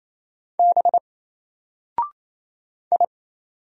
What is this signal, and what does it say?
Morse code, heard as a keyed tone, about 28 wpm, 700 Hz. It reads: B I
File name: Single-Letter-Number-28wpm